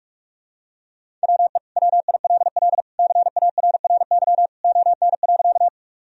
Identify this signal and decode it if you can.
Morse code, heard as a keyed tone, about 46 wpm, 700 Hz. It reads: WE WILL CARRY ON.
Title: Sentences from Top 1000 Words - Review Part 1 - 45wpm